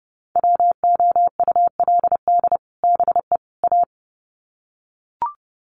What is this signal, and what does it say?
Morse code, heard as a keyed tone, about 30 wpm, 700 Hz. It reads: WOULD BE A
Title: Sets of 3 Words 30wpm